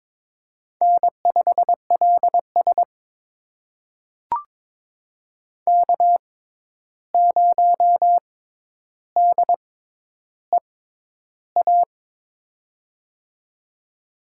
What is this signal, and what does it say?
Morse code, heard as a keyed tone, about 21 wpm, spaced as for 10 wpm, 700 Hz. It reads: N5LS K0DEA
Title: USA Call Signs - 6x Character Spacing - 22wpm